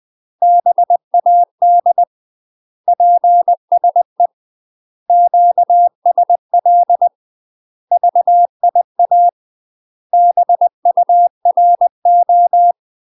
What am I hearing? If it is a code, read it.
Morse code, heard as a keyed tone, about 20 wpm, 700 Hz. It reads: BAD PSE QSL VIA BURO